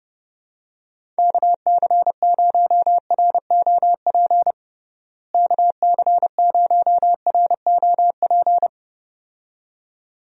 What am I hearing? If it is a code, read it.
Morse code, heard as a keyed tone, about 30 wpm, 700 Hz. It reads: KC0ROP KC0ROP
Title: USA Call Signs - 6 Characters - Repeated 2x - 30wpm